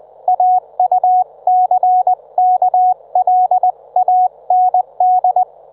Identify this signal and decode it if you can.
Morse code, heard as a keyed tone, about 20 wpm, 730 Hz, with radio noise. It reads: AUCKLAND
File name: Trivia